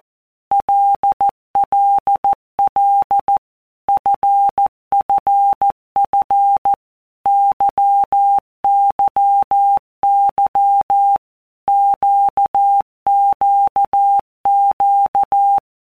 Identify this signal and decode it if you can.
Morse code, heard as a keyed tone, about 14 wpm, 800 Hz. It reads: LLL FFF YYY QQQ
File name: Basic Letter Groups By WPM's / Groups@13 / LFCYQ@13